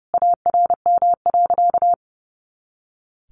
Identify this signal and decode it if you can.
Morse code, heard as a keyed tone, about 30 wpm, 700 Hz. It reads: ARM.
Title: A Sound Of Thunder 30wpm